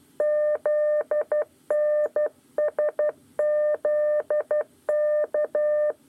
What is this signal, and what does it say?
Morse code, heard as a keyed tone, about 12 wpm, 600 Hz. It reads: ZNSZK
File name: 8 lezione cw 1